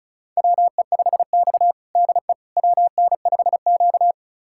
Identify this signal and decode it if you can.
Morse code, heard as a keyed tone, about 34 wpm, 700 Hz. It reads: WE5X DE WN5Q
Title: USA Callsign DE Callsign - 4 Characters - 35wpm